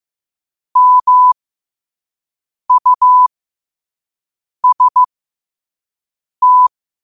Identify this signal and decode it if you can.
Morse code, heard as a keyed tone, about 15 wpm, spaced as for 5 wpm, 1000 Hz. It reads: MUST
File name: qotdfiles / 5wpm